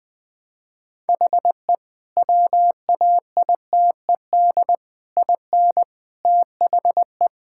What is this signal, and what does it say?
Morse code, heard as a keyed tone, about 21 wpm, 700 Hz. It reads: HE WAITED IN THE
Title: Sentences with 4-Word Phrases - Spoken Beginning - 20wpm